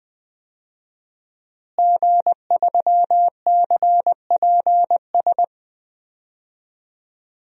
Morse code, heard as a keyed tone, about 20 wpm, 700 Hz. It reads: G3CPS
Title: DXCC-100 Call Signs - 5 Characters - Repeated 2x - 20wpm